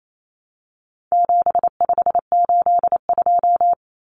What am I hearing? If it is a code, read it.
Morse code, heard as a keyed tone, about 28 wpm, 700 Hz. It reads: 7582
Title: Numbers - 4-Digit 28wpm